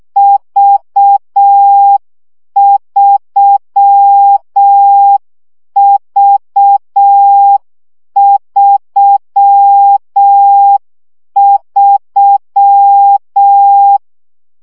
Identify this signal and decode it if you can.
Morse code, heard as a keyed tone, about 6 wpm, 800 Hz. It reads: V3V33
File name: CW gyakorlas v 3 5 baud